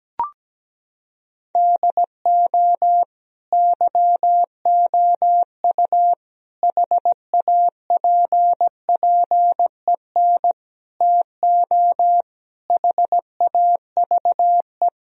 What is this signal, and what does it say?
Morse code, heard as a keyed tone, about 17 wpm, 700 Hz. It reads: DO YOU HAPPEN TO HAVE
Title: Sentences from Top 1000 Words - Review Part 2 - 17wpm